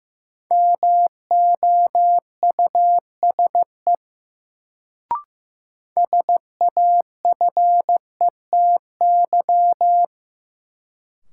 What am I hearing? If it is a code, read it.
Morse code, heard as a keyed tone, about 15 wpm, 700 Hz. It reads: MOUSE SAFETY
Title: Sentences from Top 700 Words - New Word Drill - 15wpm